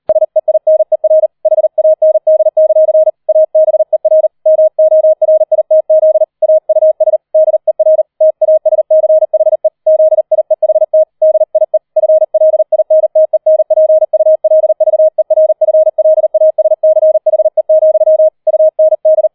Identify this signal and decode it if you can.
Morse code, heard as a keyed tone, about 38 wpm, 630 Hz. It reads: SEINER HAND; ABER MORITZ AUS DER TASCHE ZIEHT DIE FLINTENPULVERFLASCHE, UND
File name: 004 Vierter Streich 40wpm